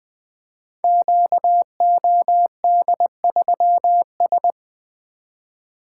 Morse code, heard as a keyed tone, about 20 wpm, 700 Hz. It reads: QOD3S
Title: Code Groups - 5 Letter:Numbers - 20wpm